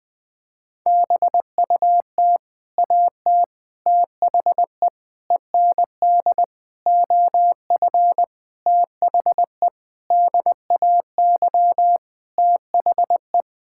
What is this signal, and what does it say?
Morse code, heard as a keyed tone, about 20 wpm, 700 Hz. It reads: BUT AT THE END OF THE DAY THE
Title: Sentences with 8-Word Phrases - 20wpm